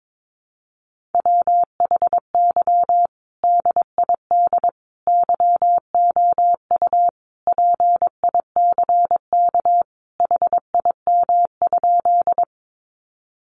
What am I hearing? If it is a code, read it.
Morse code, heard as a keyed tone, about 22 wpm, 700 Hz. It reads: WHY DID YOU PICK HIM?
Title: Sentences from Top 1000 Words 22wpm